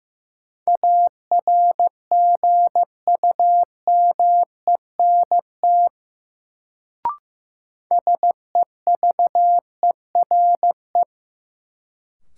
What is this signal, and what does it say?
Morse code, heard as a keyed tone, about 15 wpm, 700 Hz. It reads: ARGUMENT SEVERE
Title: Sentences from Top 1400 Words - New Word Drill - 15wpm